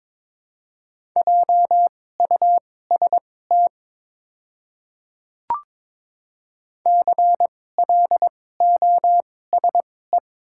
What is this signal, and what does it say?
Morse code, heard as a keyed tone, about 22 wpm, spaced as for 16 wpm, 690 Hz. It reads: JUST CLOSE
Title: 2x Character Spacing - Top 500 Words - 22wpm